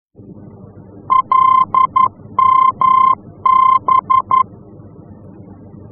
Morse code, heard as a keyed tone, about 11 wpm, 1040 Hz. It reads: LMB
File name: france / mx / LMB-F-VOR